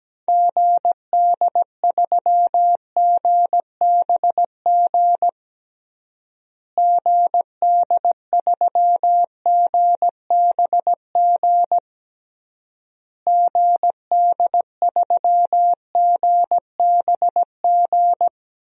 Morse code, heard as a keyed tone, about 17 wpm, 700 Hz. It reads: GD3GBG GD3GBG GD3GBG
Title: DXCC-100 Call Signs - 6 Characters - Repeated 3x - 17wpm